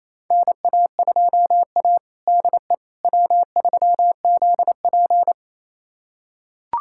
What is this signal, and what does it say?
Morse code, heard as a keyed tone, about 27 wpm, 700 Hz. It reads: NA2A DE W3ZP
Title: USA Callsign DE Callsign - 4 Characters - 28wpm